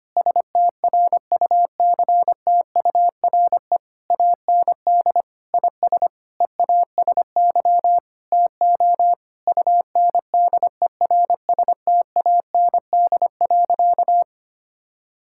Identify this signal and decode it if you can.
Morse code, heard as a keyed tone, about 25 wpm, 700 Hz. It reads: STRUCTURE AND IS EASY TO UNDERSTAND.
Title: Sentences from Top 1000 Words - Review Part 6 - 25wpm